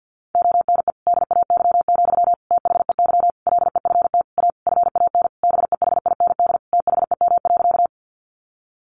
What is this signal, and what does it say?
Morse code, heard as a keyed tone, about 50 wpm, 700 Hz. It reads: ONE DAY, THEY LEFT A FAN BEHIND THEM.
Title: Sentences from Top 1300 Words 50wpm